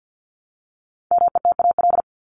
Morse code, heard as a keyed tone, about 48 wpm, 700 Hz. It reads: METAL
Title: Top 751-1000 Words 50wpm